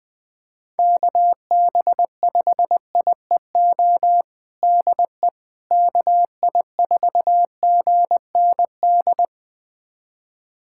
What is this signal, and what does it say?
Morse code, heard as a keyed tone, about 21 wpm, 700 Hz. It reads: KB5IEO DE KI4GND
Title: USA Callsign DE Callsign 20wpm